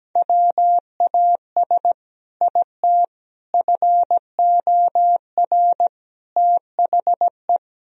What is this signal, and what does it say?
Morse code, heard as a keyed tone, about 17 wpm, 700 Hz. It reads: WAS IT FOR THE
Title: Sentences from Top 1000 Words - Review Part 3 - 17wpm